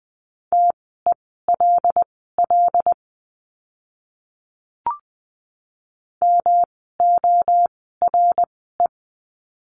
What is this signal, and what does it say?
Morse code, heard as a keyed tone, about 20 wpm, spaced as for 15 wpm, 700 Hz. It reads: TELL MORE
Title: 108 Lesson 18f - Letter m - Words 20wpm